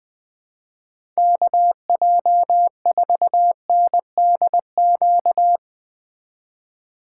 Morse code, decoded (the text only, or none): KJ4NDQ